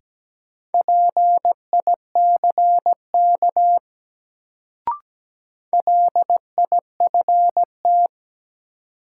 Morse code, heard as a keyed tone, about 17 wpm, 700 Hz. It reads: PICK LIFT
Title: Sentences from Top 1000 Words - New Word Drill - 17wpm